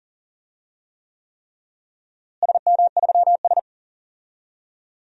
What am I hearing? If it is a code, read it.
Morse code, heard as a keyed tone, about 40 wpm, 700 Hz. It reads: SM3S
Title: DXCC-100 Call Signs - 4 Characters - Repeated 3x - 40wpm